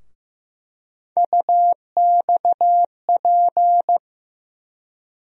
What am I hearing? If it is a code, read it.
Morse code, heard as a keyed tone, about 15 wpm, 700 Hz. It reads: UXP
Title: USA Call Sign Postfixes - 3 Letters 15wpm